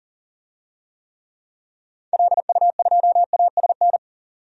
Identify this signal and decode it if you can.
Morse code, heard as a keyed tone, about 40 wpm, 700 Hz. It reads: LU2ASN
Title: DXCC-100 Call Signs - 6 Characters - 40wpm